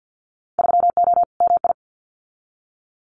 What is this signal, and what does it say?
Morse code, heard as a keyed tone, about 50 wpm, 700 Hz. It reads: 3O MI
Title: Winter Field Day Exchanges Encore 50wpm